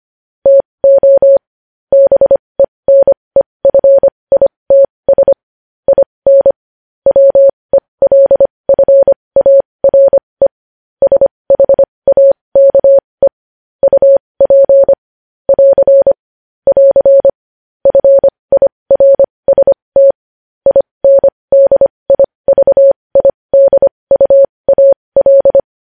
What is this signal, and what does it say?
Morse code, heard as a keyed tone, about 25 wpm, spaced as for 20 wpm, 550 Hz. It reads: TO BENEFITS IN WELFARE SHAKE UP <AR> <AR> FIRST INDIVIDUAL
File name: podcast / News headlines-1x-25x20